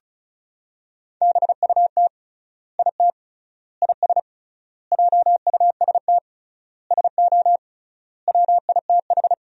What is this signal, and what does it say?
Morse code, heard as a keyed tone, about 34 wpm, 700 Hz. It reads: BUT IT IS JUST SO WITH
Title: 3x Word-Spacing - Sentences from Top 200 Words - 35wpm